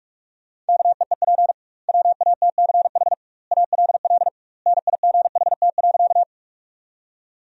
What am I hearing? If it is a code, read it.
Morse code, heard as a keyed tone, about 46 wpm, 700 Hz. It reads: KEEP WATCH ALL NIGHT.
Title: 2x Word-Spacing - Sentences from Top 300 Words - 45wpm